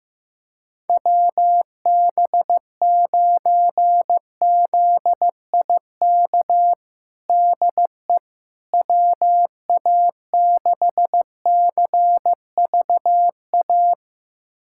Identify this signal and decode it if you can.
Morse code, heard as a keyed tone, about 16 wpm, 700 Hz. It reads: WB9ZIK DE WA6CVA
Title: USA Callsign DE Callsign - 6 Characters - 15wpm